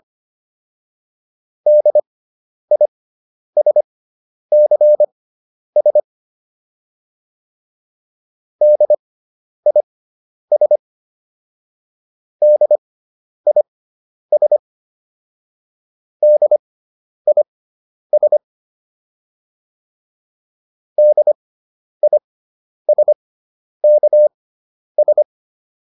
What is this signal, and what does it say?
Morse code, heard as a keyed tone, about 25 wpm, spaced as for 10 wpm, 600 Hz. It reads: DISCS DIS DIS DIS DISKS